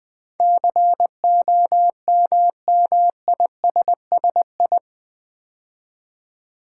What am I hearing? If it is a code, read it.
Morse code, heard as a keyed tone, about 20 wpm, 700 Hz. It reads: COMMISSI
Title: Build A Word - 4 Letters At A Time - 12-Letter Words - 20wpm